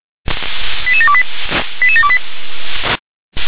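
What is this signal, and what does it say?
Five-tone selective call (ZVEI), heard as quick runs of five tones, tones of about 70 ms. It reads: 88728 88728
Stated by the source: Received by SDR, ZVEI five-tone call.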